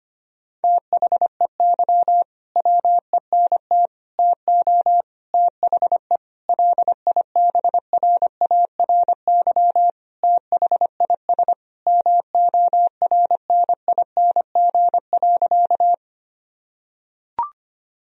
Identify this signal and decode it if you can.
Morse code, heard as a keyed tone, about 25 wpm, 700 Hz. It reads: THEY WENT TO THE LIBRARY THIS MORNING.
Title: Sentences from Top 1000 Words - Review Part 4 - 25wpm